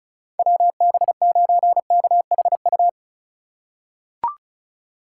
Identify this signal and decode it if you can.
Morse code, heard as a keyed tone, about 34 wpm, 700 Hz. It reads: WB9KHU
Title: USA Call Signs - 6 Characters - 2x Character Spacing - 35wpm